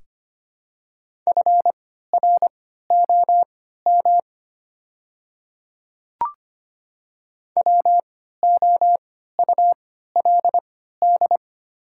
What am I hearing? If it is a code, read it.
Morse code, heard as a keyed tone, about 25 wpm, spaced as for 14 wpm, 700 Hz. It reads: FROM WOULD
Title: 3x Character Spacing - Top 200 Words - 25wpm